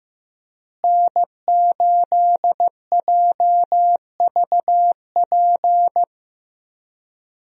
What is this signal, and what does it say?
Morse code, heard as a keyed tone, about 15 wpm, 700 Hz. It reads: N8JVP
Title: USA Call Signs - 5 Characters - Repeated 2x - 15wpm